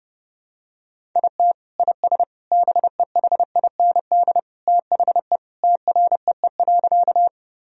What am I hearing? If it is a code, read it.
Morse code, heard as a keyed tone, about 30 wpm, 700 Hz. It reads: IT IS BEHIND THE TREE.
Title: Sentences from Top 1000 Words - Review Part 1 - 30wpm